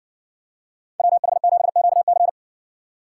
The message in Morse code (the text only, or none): WH7QC